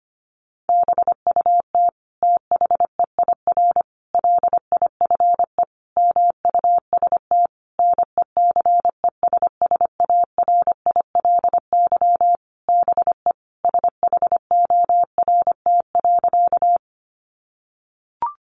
Morse code, heard as a keyed tone, about 25 wpm, 700 Hz. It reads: BUT THEIR LIFE MUST NECESSARILY BE SHORT.
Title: Sentences from Top 1400 Words 25wpm